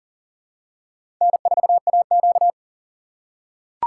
Morse code, heard as a keyed tone, about 40 wpm, 700 Hz. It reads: N4AQ
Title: USA Call Signs - 4 Characters - 4x Character Spacing - 40wpm